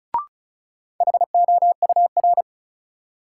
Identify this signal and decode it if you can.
Morse code, heard as a keyed tone, about 36 wpm, 700 Hz. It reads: HOUR
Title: Sentences from Top 400 Words - New Word Drill - 35wpm